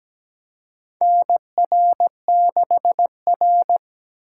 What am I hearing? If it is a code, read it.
Morse code, heard as a keyed tone, about 17 wpm, 700 Hz. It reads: NR6R
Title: USA Call Signs - 4 Characters - 4x Character Spacing - 17wpm